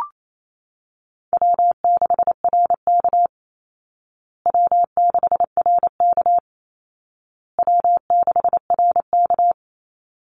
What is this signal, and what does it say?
Morse code, heard as a keyed tone, about 28 wpm, 700 Hz. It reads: W6RK W6RK W6RK
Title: Call Signs Repeated 2x - 28wpm